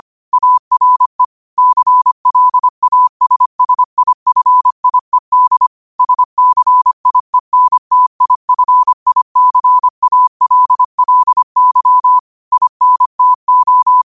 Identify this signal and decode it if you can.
Morse code, heard as a keyed tone, about 25 wpm, 1000 Hz. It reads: ARE CLASSIFIED SCIENTIFICALLY INTO
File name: qotdfiles / 25wpm